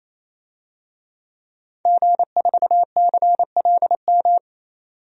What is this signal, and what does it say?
Morse code, heard as a keyed tone, about 27 wpm, 700 Hz. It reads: G4CLM
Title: DXCC-100 Call Signs - 5 Characters - 28wpm